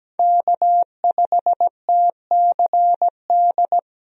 Morse code, heard as a keyed tone, about 17 wpm, 700 Hz. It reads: K5TCD
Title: USA Call Signs - 5 Characters - Repeated 2x - 17wpm